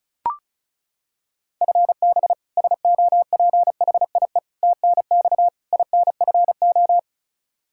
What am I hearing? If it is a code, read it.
Morse code, heard as a keyed tone, about 36 wpm, 700 Hz. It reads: FB SOPHIE TNX INFO
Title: Practice QSO - Exchange 3 - More Info - 35wpm